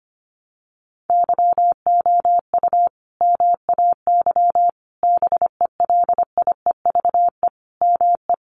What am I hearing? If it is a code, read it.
Morse code, heard as a keyed tone, about 25 wpm, 700 Hz. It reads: YOU MAY BELIEVE ME
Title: Sentences from Top 1000 Words 25wpm